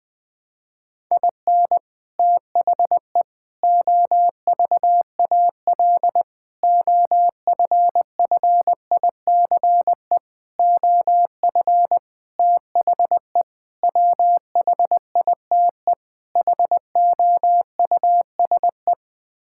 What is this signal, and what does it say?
Morse code, heard as a keyed tone, about 20 wpm, 700 Hz. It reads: IN THE OVAL OFFICE OF THE WHITE HOUSE
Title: Sets of 8 Words - 20wpm